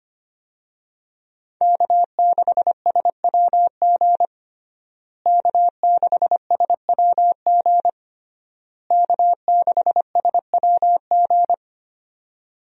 Morse code, heard as a keyed tone, about 25 wpm, 700 Hz. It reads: K6SWG K6SWG K6SWG
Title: USA Call Signs - 5 Characters - Repeated 3x - 25wpm